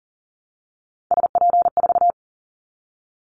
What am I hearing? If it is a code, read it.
Morse code, heard as a keyed tone, about 40 wpm, 700 Hz. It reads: SP4
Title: Callsign Prefixes for Top 25 DXCC - Repeated 3x - 40wpm